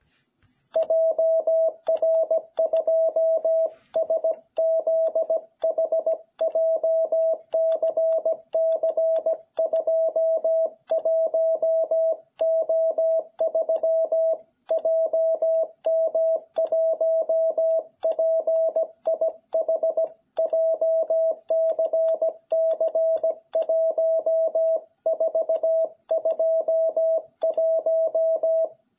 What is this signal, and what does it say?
Morse code, heard as a keyed tone, about 17 wpm, 660 Hz. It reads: JR2SZH JCC21O3 JM1PIH JCC1421